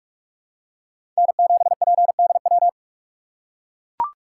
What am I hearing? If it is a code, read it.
Morse code, heard as a keyed tone, about 44 wpm, 700 Hz. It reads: N7PDW
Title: USA Call Signs - 2x Character Spacing - 45wpm